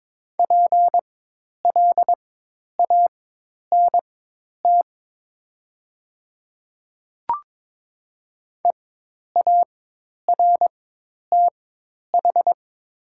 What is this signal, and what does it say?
Morse code, heard as a keyed tone, about 22 wpm, spaced as for 10 wpm, 700 Hz. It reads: PLANT EARTH
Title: 4x Character Spacing - Top 300 Words - 22wpm